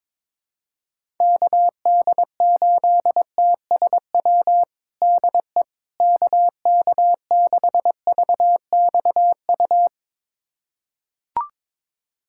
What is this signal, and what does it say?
Morse code, heard as a keyed tone, about 22 wpm, 700 Hz. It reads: KD8TSW DE KK6VXU
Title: USA Callsign DE Callsign - 6 Characters - 22wpm